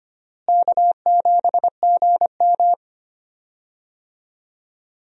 Morse code, heard as a keyed tone, about 25 wpm, 700 Hz. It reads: K7GM
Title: USA Call Signs - 4 Characters - Repeated 2x - 25wpm